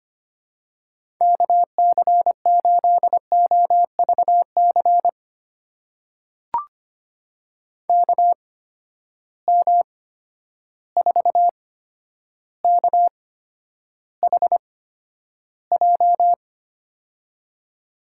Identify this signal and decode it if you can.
Morse code, heard as a keyed tone, about 25 wpm, spaced as for 11 wpm, 700 Hz. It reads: KC8OVC KM4KHJ
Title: USA Call Signs - 6 Characters - 8x Character Spacing - 25wpm